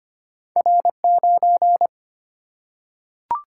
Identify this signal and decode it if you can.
Morse code, heard as a keyed tone, about 25 wpm, 700 Hz. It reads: R 9
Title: Single-Letter Single-Number 25wpm